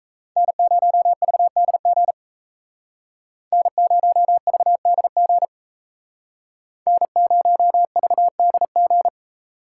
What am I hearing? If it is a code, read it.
Morse code, heard as a keyed tone, about 34 wpm, 700 Hz. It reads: N0VDG N0VDG N0VDG
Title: Speed Racing - USA 1x3 Call Signs 28wpm